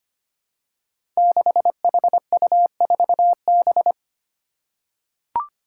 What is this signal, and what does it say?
Morse code, heard as a keyed tone, about 25 wpm, 700 Hz. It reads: 6HU4B
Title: Five Challenging Letter-Number Combinations 25wpm